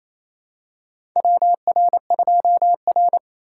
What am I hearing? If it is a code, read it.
Morse code, heard as a keyed tone, about 28 wpm, 700 Hz. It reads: WR2R
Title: USA Call Signs - 4 Characters - 3x Character Spacing - 28wpm